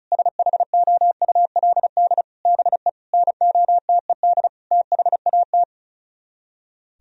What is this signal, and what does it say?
Morse code, heard as a keyed tone, about 45 wpm, 700 Hz. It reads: SHOULD BE NOTED THAT